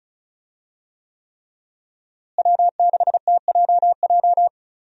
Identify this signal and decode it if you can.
Morse code, heard as a keyed tone, about 34 wpm, 700 Hz. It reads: W6TJJ
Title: USA Call Signs - 5 Characters - Repeated 3x - 35wpm